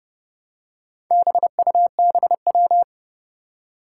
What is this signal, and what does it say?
Morse code, heard as a keyed tone, about 29 wpm, 700 Hz. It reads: BUBW
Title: Four Challenging Letter-Number Combinations 30wpm